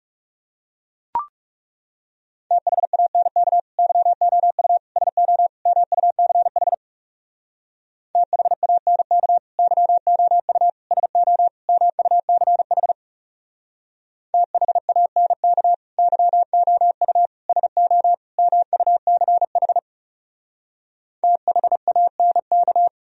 Morse code, heard as a keyed tone, about 37 wpm, 700 Hz. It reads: THANK YOU SO MUCH THANK YOU SO MUCH THANK YOU SO MUCH THANK
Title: Speed-Racing - Sets of 4 Words 30wpm